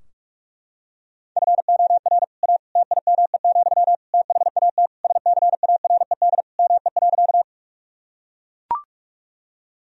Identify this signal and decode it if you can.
Morse code, heard as a keyed tone, about 41 wpm, 700 Hz. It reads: FOR A TIME, THAT SCARED ME.
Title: Sentences from Top 1000 Words - Review Part 4 - 45wpm